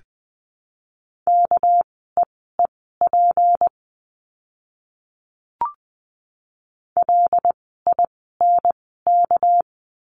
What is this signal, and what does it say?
Morse code, heard as a keyed tone, about 20 wpm, spaced as for 15 wpm, 700 Hz. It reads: KEEP LINK
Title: 199 Lesson 32f - Letter k - Words 20wpm